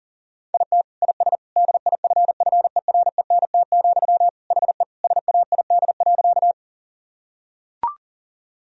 Morse code, heard as a keyed tone, about 40 wpm, 700 Hz. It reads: IT IS DIFFERENT, HE SAID.
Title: Sentences from Top 1000 Words - Review Part 3 - 40wpm